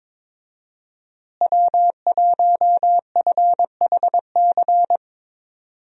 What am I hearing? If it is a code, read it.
Morse code, heard as a keyed tone, about 22 wpm, 700 Hz. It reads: W1FHC
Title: USA Call Signs - 5 Characters - Repeated 3x - 22wpm